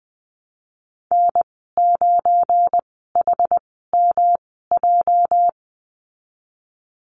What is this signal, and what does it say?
Morse code, heard as a keyed tone, about 20 wpm, spaced as for 14 wpm, 700 Hz. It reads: N9HMJ
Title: 207 Lesson 33g - Letter j - Call Signs 20wpm